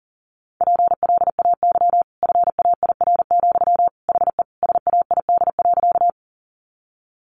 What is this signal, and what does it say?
Morse code, heard as a keyed tone, about 40 wpm, 700 Hz. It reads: PLAY FAIR, HE SAID.
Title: Sentences from Top 1300 Words 40wpm